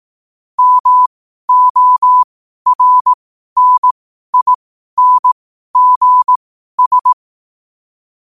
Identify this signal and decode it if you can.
Morse code, heard as a keyed tone, about 18 wpm, spaced as for 12 wpm, 1000 Hz. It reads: MORNINGS
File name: qotdfiles / 12wpm